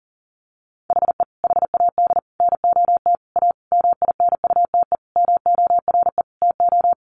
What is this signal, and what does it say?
Morse code, heard as a keyed tone, about 40 wpm, 700 Hz. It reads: HE HAD NOT A MINUTE MORE TO
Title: Sentences from Top 1100 Words 40wpm